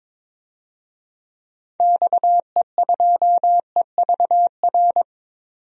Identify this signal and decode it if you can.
Morse code, heard as a keyed tone, about 22 wpm, 700 Hz. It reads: XE2EVR
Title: DXCC-100 Call Signs - 6 Characters - 22wpm